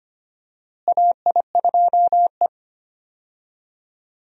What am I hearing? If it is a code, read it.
Morse code, heard as a keyed tone, about 25 wpm, 700 Hz. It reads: AI2E
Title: USA Call Signs - 4 Characters - Repeated 3x - 25wpm